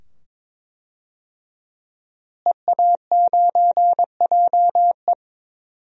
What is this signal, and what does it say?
Morse code, heard as a keyed tone, about 22 wpm, 700 Hz. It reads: EA9JE
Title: DXCC-100 Call Signs - 5 Characters - Repeated 2x - 22wpm